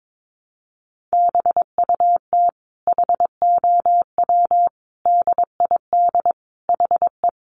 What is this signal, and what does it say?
Morse code, heard as a keyed tone, about 22 wpm, 700 Hz. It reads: BUT HOW DID HE
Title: Sentences from Top 900 Words 22wpm